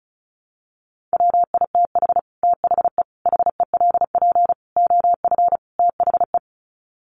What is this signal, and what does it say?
Morse code, heard as a keyed tone, about 34 wpm, 700 Hz. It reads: WITH THE HELP OF THE
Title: Sets of 5 Words Encore 35wpm